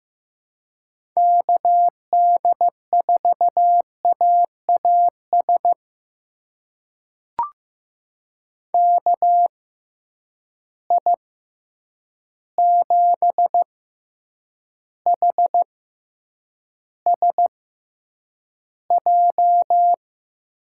Morse code, heard as a keyed tone, about 15 wpm, spaced as for 8 wpm, 700 Hz. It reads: KD4AAS KI7HSJ
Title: USA Call Signs - 6 Characters - 6x Character Spacing - 15wpm